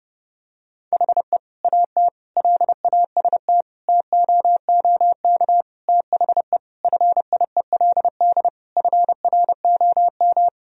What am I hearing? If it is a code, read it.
Morse code, heard as a keyed tone, about 32 wpm, 700 Hz. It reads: HE AT LAST TOOK THE FIELD FROM
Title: Sentences from Top 1000 Words - Review Part 2 - 30wpm